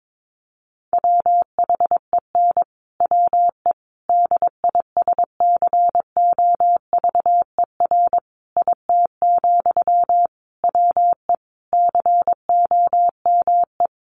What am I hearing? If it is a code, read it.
Morse code, heard as a keyed tone, about 22 wpm, 700 Hz. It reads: WHEN WE DISCOVER IT, WE COME
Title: Sentences from Top 1100 Words 22wpm